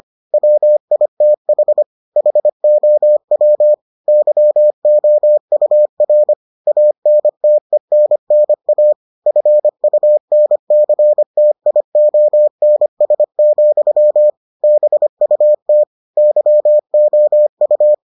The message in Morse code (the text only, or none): WITH HOW YOUR ANTENNA FUNCTIONS, BUT YOU